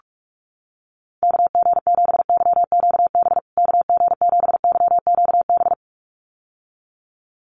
Morse code, heard as a keyed tone, about 44 wpm, 700 Hz. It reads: KG7YQB KG7YQB
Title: USA Call Signs Repeated 2x 45wpm